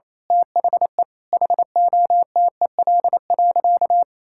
Morse code, heard as a keyed tone, about 28 wpm, 700 Hz. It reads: THE HOTEL.
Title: Sentences from Top 1600 Words 28wpm